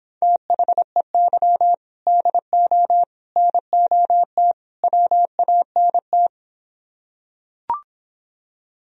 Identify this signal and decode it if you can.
Morse code, heard as a keyed tone, about 26 wpm, 700 Hz. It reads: THEY DO NOT WANT